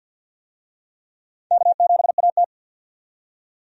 Morse code, heard as a keyed tone, about 53 wpm, 700 Hz. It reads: K7AT
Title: USA Call Signs - 4 Characters - 50wpm